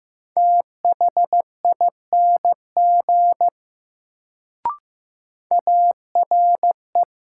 Morse code, heard as a keyed tone, about 15 wpm, 700 Hz. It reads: THING ARE
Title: Sentences from Top 100 Words - New Word Drill - 15wpm